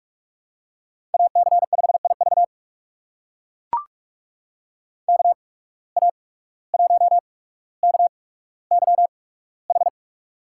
Morse code, heard as a keyed tone, about 44 wpm, spaced as for 18 wpm, 700 Hz. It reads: AC5IV KA1KYH
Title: USA Call Signs - 8x Character Spacing - 45wpm